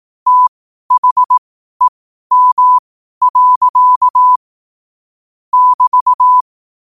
Morse code, heard as a keyed tone, about 18 wpm, spaced as for 12 wpm, 1000 Hz. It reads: THEM. <BT>
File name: 12wpm